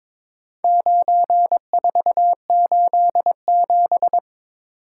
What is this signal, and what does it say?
Morse code, heard as a keyed tone, about 23 wpm, 700 Hz. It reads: 9487
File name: practice / 22wpm / Four Letter-Number Combinations - 22wpm